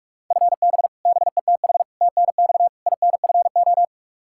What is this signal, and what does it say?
Morse code, heard as a keyed tone, about 46 wpm, 700 Hz. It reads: FB BETH TNX INFO
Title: Practice QSO - Exchange 3 - More Info - 45wpm